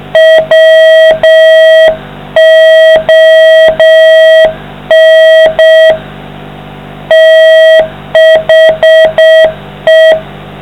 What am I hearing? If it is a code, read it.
Morse code, heard as a keyed tone, about 6 wpm, 660 Hz, with noise. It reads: WON THE